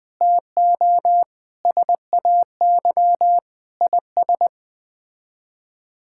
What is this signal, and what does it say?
Morse code, heard as a keyed tone, about 20 wpm, 700 Hz. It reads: TO SAY IS
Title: Sets of 8 Words - 20wpm